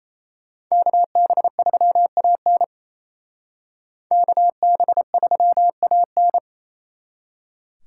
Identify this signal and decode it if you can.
Morse code, heard as a keyed tone, about 36 wpm, 700 Hz. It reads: KB3AN KB3AN